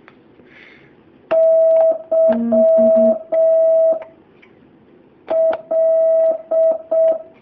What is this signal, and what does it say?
Morse code, heard as a keyed tone, about 6 wpm, spaced as for 4 wpm, 650 Hz. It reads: YL
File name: jh8cbh / 2019cw